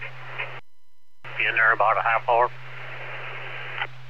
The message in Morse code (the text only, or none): none